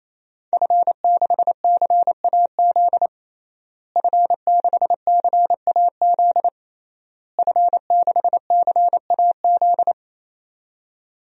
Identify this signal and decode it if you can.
Morse code, heard as a keyed tone, about 28 wpm, 700 Hz. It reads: F6CAZ F6CAZ F6CAZ
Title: DXCC-100 Call Signs - 5 Characters - Repeated 3x - 28wpm